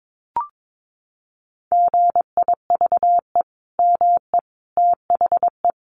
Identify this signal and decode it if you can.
Morse code, heard as a keyed tone, about 22 wpm, 700 Hz. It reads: GIVE ME THE
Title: Sentences from Top 1100 Words 22wpm